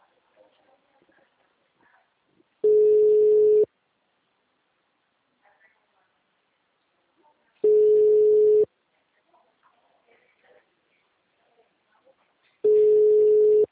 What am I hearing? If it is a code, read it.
Ringback tone (Europe, 425 Hz).